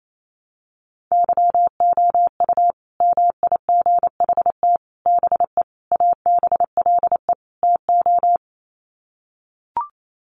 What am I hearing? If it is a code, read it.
Morse code, heard as a keyed tone, about 28 wpm, 700 Hz. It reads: YOU MIGHT BE ABLE TO
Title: Sets of 5 Words Encore2 28wpm